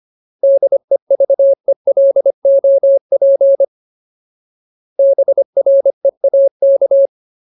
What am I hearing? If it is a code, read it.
Morse code, heard as a keyed tone, about 25 wpm, 550 Hz. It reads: DEVELOP BREAK
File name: Ultimate Word List / 25WPM Ultimate